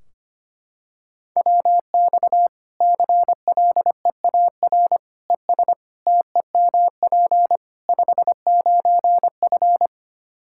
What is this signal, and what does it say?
Morse code, heard as a keyed tone, about 25 wpm, 700 Hz. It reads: WX CLEAR ES TEMP 59F
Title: K1USN SST QSO Practice 25wpm